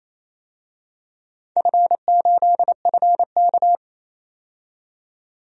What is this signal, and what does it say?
Morse code, heard as a keyed tone, about 28 wpm, 700 Hz. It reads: F8FK
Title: DXCC-100 Call Signs - 4 Characters - 28wpm